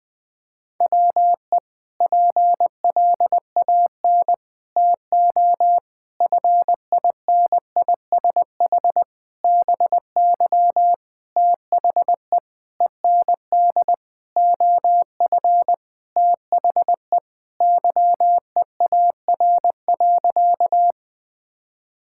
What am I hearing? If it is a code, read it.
Morse code, heard as a keyed tone, about 20 wpm, 700 Hz. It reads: WE PLAN TO FINISH BY THE END OF THE YEAR.
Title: Sentences with 5-Word Phrases - Spoken Beginning - 20wpm